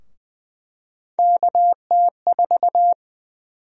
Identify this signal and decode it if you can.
Morse code, heard as a keyed tone, about 20 wpm, 700 Hz. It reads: KT4
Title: USA Call Sign Prefixes - 20wpm